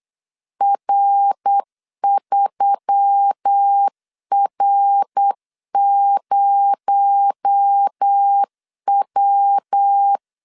Morse code, heard as a keyed tone, about 8 wpm, 800 Hz. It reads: R3R0W
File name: CW / DET / 1 6 1500C S B 8wpm 5142